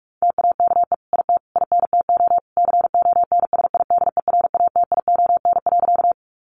Morse code, heard as a keyed tone, about 44 wpm, 700 Hz. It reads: TAKE IT INTO CONSIDERATION.
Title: Sentences from Top 1500 Words 45wpm